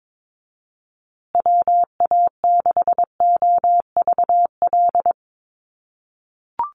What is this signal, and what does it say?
Morse code, heard as a keyed tone, about 22 wpm, 700 Hz. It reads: WA6OVL
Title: USA Call Signs Repeated 2x 22wpm